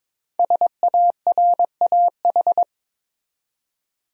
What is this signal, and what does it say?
Morse code, heard as a keyed tone, about 22 wpm, 700 Hz. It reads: SARAH
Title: USA Common Names - Max 5 Letters - 22wpm